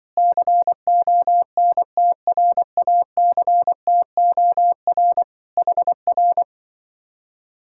Morse code, heard as a keyed tone, about 24 wpm, 700 Hz. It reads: CONTRACTOR HR
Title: Speed-Racing Common QSO Phrases Encore - 20wpm